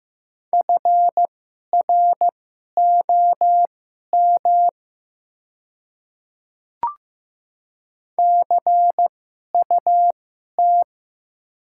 Morse code, heard as a keyed tone, about 15 wpm, spaced as for 11 wpm, 700 Hz. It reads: FROM CUT
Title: 2x Character Spacing - Top 300 Words - 15wpm